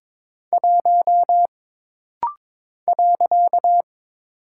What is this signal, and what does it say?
Morse code, heard as a keyed tone, about 22 wpm, 700 Hz. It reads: 1 .
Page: This rapid-fire timing is geared to push your ICR (Instant Character Recognition) skills.